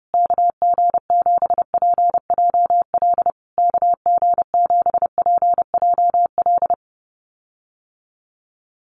Morse code, heard as a keyed tone, about 30 wpm, 700 Hz. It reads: KG7PJL KG7PJL
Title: USA Call Signs Repeated 3x 30wpm